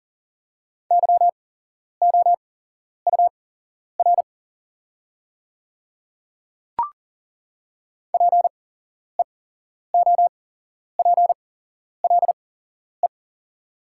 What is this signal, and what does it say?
Morse code, heard as a keyed tone, about 40 wpm, spaced as for 11 wpm, 700 Hz. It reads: YOUR PEOPLE
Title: 8x Character Spacing - Top 100 Words - 40wpm